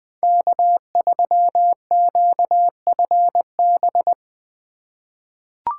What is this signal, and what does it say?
Morse code, heard as a keyed tone, about 20 wpm, 700 Hz. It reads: K3QFB
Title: USA Call Signs - 5 Characters - 4x Character Spacing - 20wpm